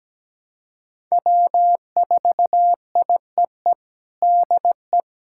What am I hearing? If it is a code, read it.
Morse code, heard as a keyed tone, about 17 wpm, 700 Hz. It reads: W4IEE DE
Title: USA Callsign DE Callsign - 5 Characters - 17wpm